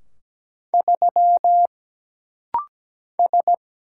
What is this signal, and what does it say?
Morse code, heard as a keyed tone, about 17 wpm, 700 Hz. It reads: 3 S
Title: Single Letter-Number Alt - Rapid-Fire 17wpm